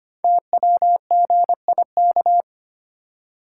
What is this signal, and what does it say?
Morse code, heard as a keyed tone, about 25 wpm, 700 Hz. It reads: TWGIK
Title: Code Groups - 5 Letters - 25wpm